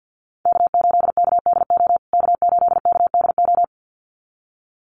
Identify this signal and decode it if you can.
Morse code, heard as a keyed tone, about 50 wpm, 700 Hz. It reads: K8KDO K8KDO
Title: USA Call Signs Repeated 2x 50wpm